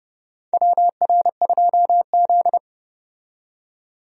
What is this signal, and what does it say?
Morse code, heard as a keyed tone, about 30 wpm, 700 Hz. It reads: WR2Z
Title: USA Call Signs - 4 Characters - 30wpm